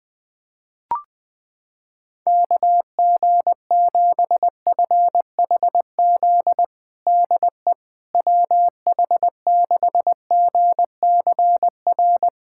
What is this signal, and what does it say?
Morse code, heard as a keyed tone, about 21 wpm, 700 Hz. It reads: KG7FHZ DE WH6GCR
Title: USA Callsign DE Callsign - 6 Characters - 20wpm